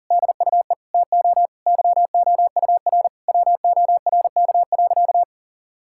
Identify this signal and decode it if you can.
Morse code, heard as a keyed tone, about 40 wpm, 700 Hz. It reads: DUE TO YOUR WORK.
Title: Sentences from Top 1000 Words - Review Part 6 - 40wpm